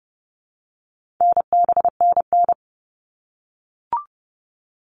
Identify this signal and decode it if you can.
Morse code, heard as a keyed tone, about 30 wpm, 700 Hz. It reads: NBNN
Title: All Cut Numbers - 4-Digit 30wpm